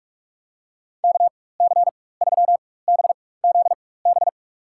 Morse code, heard as a keyed tone, about 45 wpm, spaced as for 21 wpm, 700 Hz. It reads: KC3BZB
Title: USA Call Signs - 6 Characters - 4x Character Spacing - 45wpm